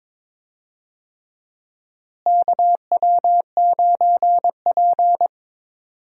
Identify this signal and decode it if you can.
Morse code, heard as a keyed tone, about 22 wpm, 700 Hz. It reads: KW9P
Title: USA Call Signs - 4 Characters - Repeated 3x - 22wpm